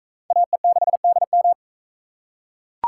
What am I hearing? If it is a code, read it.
Morse code, heard as a keyed tone, about 43 wpm, 700 Hz. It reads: AE6DM